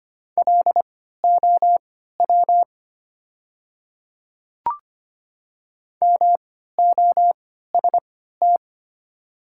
Morse code, heard as a keyed tone, about 25 wpm, spaced as for 14 wpm, 700 Hz. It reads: LOW MOST
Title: 3x Character Spacing - Top 200 Words - 25wpm